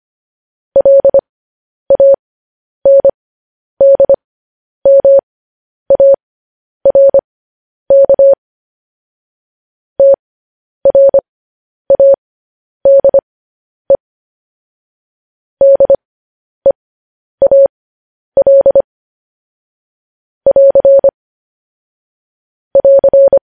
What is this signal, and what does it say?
Morse code, heard as a keyed tone, about 25 wpm, spaced as for 10 wpm, 550 Hz. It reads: LANDMARK TRADE DEAL <AR> <AR>
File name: podcast / News headlines-1x-25x10